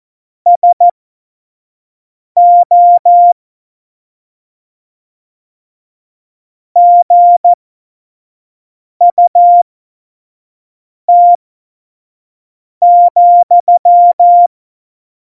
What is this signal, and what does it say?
Morse code, heard as a keyed tone, about 14 wpm, spaced as for 5 wpm, 700 Hz. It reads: SO GUT,
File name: Morsekurs / Witze / Witz 27 Juni 17, Tempo 25 F70